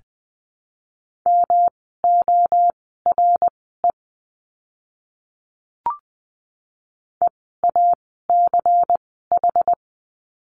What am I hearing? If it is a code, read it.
Morse code, heard as a keyed tone, about 20 wpm, spaced as for 15 wpm, 700 Hz. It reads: MORE EACH
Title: 192 Lesson 31f - Letter v - Words 20wpm